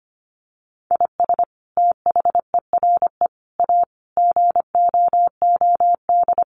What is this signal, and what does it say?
Morse code, heard as a keyed tone, about 25 wpm, 700 Hz. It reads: IS THERE A GOOD
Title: Sentences from Top 1400 Words 25wpm